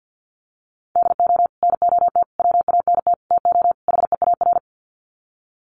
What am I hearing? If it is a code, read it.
Morse code, heard as a keyed tone, about 50 wpm, 700 Hz. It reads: DO NOT WANT TO HEAR
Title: Sets of 5 Words 50wpm